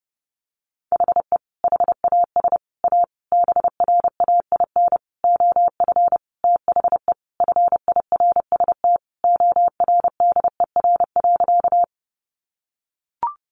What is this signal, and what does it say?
Morse code, heard as a keyed tone, about 30 wpm, 700 Hz. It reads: HE HAS A BRAIN OF THE FIRST ORDER.
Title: Sentences from Top 1100 Words 30wpm